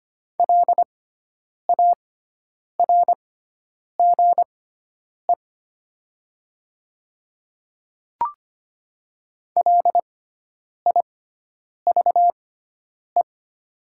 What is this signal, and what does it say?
Morse code, heard as a keyed tone, about 25 wpm, spaced as for 9 wpm, 700 Hz. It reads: LARGE LIVE
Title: 6x Character Spacing - Top 200 Words - 25wpm